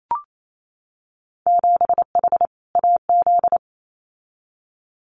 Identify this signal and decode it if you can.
Morse code, heard as a keyed tone, about 28 wpm, 700 Hz. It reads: 7H AZ
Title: Winter Field Day Exchanges Encore 28wpm